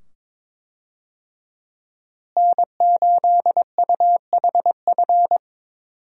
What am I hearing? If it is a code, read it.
Morse code, heard as a keyed tone, about 21 wpm, 700 Hz. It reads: N8UHF
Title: USA Call Signs - 5 Characters - Repeated 2x - 22wpm